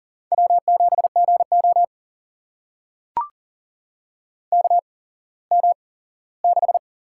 Morse code, heard as a keyed tone, about 40 wpm, spaced as for 19 wpm, 700 Hz. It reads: W7GO KM6
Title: USA Call Signs - 8x Character Spacing - 40wpm